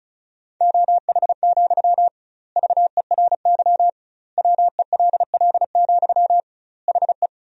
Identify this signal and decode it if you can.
Morse code, heard as a keyed tone, about 36 wpm, 700 Hz. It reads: OH, VERY WELL, HE
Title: 2x Word-Spacing - Sentences from Top 500 Words - 35wpm